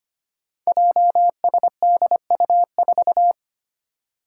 Morse code, heard as a keyed tone, about 25 wpm, 700 Hz. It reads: JSDU4
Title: Five Challenging Letter-Number Combinations 25wpm